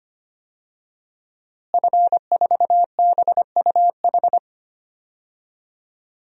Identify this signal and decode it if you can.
Morse code, heard as a keyed tone, about 25 wpm, 700 Hz. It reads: F4BUH
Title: DXCC-100 Call Signs - 5 Characters - Repeated 3x - 25wpm